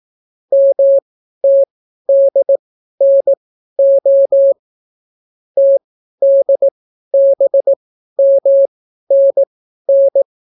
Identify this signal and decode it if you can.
Morse code, heard as a keyed tone, about 18 wpm, spaced as for 12 wpm, 550 Hz. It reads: MTDNO TDBMNN